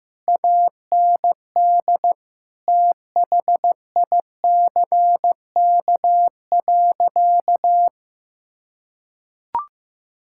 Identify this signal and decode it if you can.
Morse code, heard as a keyed tone, about 15 wpm, 700 Hz. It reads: AND THICK.
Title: Sentences from Top 1600 Words 15wpm